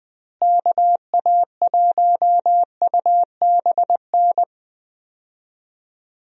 Morse code, heard as a keyed tone, about 20 wpm, 700 Hz. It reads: KA1UBN
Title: USA Call Signs - 6 Characters - Repeated 2x - 20wpm